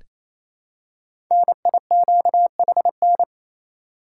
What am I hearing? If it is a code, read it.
Morse code, heard as a keyed tone, about 27 wpm, 700 Hz. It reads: NIQHN
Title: Code Groups - 5 Letter:Numbers - 28wpm